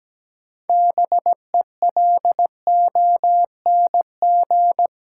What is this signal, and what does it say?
Morse code, heard as a keyed tone, about 17 wpm, 700 Hz. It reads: BELONG
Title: Six Letter Words - 17wpm